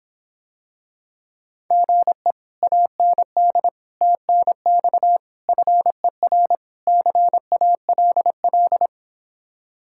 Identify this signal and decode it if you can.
Morse code, heard as a keyed tone, about 26 wpm, 700 Hz. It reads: GE AND TNX FER CALL